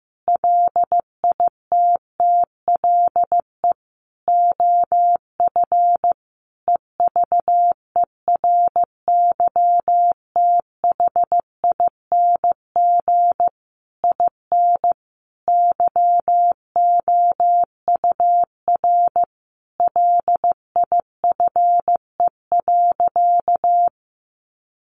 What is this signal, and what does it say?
Morse code, heard as a keyed tone, about 14 wpm, 700 Hz. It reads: LITTLE OF EVERYTHING IN YOUR LIFE.
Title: Sentences from Top 700 Words 15wpm